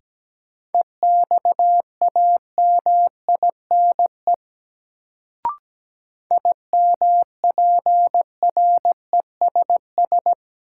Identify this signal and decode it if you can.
Morse code, heard as a keyed tone, about 17 wpm, 700 Hz. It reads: EXAMINE IMPRESS
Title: Sentences from Top 1600 Words - New Word Drill Alt - 17wpm